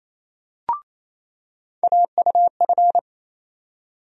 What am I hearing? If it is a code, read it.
Morse code, heard as a keyed tone, about 28 wpm, 700 Hz. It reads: AUF
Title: USA Call Sign Postfixes - 3 Letters 28wpm